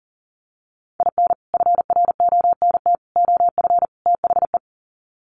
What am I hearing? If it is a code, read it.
Morse code, heard as a keyed tone, about 37 wpm, 700 Hz. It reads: IN FRONT OF THE
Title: Sets of 4 Words 40wpm